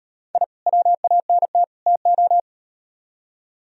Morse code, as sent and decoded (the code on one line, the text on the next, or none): .. / .-- .- -. - / - ---
I WANT TO